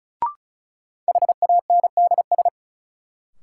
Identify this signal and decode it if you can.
Morse code, heard as a keyed tone, about 36 wpm, 700 Hz. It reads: HANDS
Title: Sentences from Top 1000 Words - Review Part 6 - New Word Drill - 35wpm